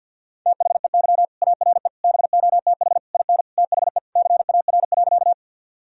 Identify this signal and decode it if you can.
Morse code, heard as a keyed tone, about 51 wpm, 700 Hz. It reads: THEY ARE BOTH IN THE CAR.
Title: Sentences from Top 1000 Words - Review Part 1 - 50wpm